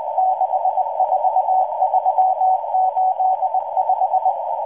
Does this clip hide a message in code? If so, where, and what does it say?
Yes: Morse code from 0.2 to 4.3 s, heard as a keyed tone, about 19 wpm, 750 Hz, with noise. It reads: 64385